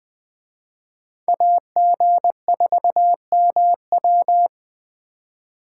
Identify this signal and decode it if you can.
Morse code, heard as a keyed tone, about 24 wpm, 700 Hz. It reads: AG4MW
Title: Speed Racing - USA 2x2 Call Signs 20wpm